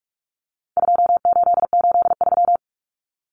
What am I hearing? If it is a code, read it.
Morse code, heard as a keyed tone, about 44 wpm, 700 Hz. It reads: 2883
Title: Common Cut Numbers - 4-Digit 45wpm